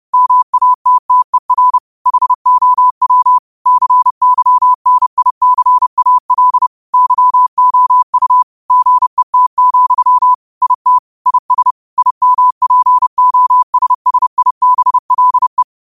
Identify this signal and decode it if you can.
Morse code, heard as a keyed tone, about 30 wpm, 1000 Hz. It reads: MATTER HOW CYNICAL YOU GET, IT IS IMPOSSIBLE
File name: qotdfiles / 30wpm